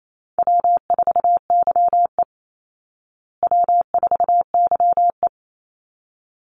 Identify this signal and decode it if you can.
Morse code, heard as a keyed tone, about 28 wpm, 700 Hz. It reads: W4YE W4YE
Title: Easy Call Signs Repeated 2x - 28wpm